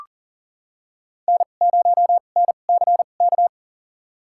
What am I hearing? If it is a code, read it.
Morse code, heard as a keyed tone, about 40 wpm, spaced as for 29 wpm, 700 Hz. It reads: N0NCK
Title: USA Call Signs - 5 Characters - 2x Character Spacing - 40wpm